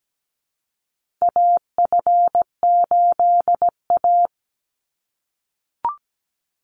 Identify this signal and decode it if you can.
Morse code, heard as a keyed tone, about 17 wpm, 700 Hz. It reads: AF8A
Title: Call Signs - 17wpm